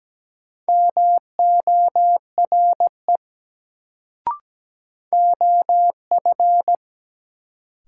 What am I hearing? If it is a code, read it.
Morse code, heard as a keyed tone, about 17 wpm, 700 Hz. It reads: MORE OF
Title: Sets of 2 Words - New Word Drill - 17wpm